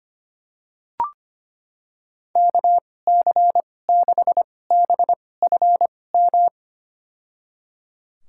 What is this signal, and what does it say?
Morse code, heard as a keyed tone, about 25 wpm, spaced as for 18 wpm, 700 Hz. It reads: KC6BFM
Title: USA Call Signs - 6 Characters - 2x Character Spacing - 25wpm